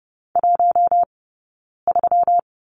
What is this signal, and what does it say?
Morse code, heard as a keyed tone, about 30 wpm, 700 Hz. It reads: 1 3
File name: Lesson 20d - Number 3 - Two Characters 30wpm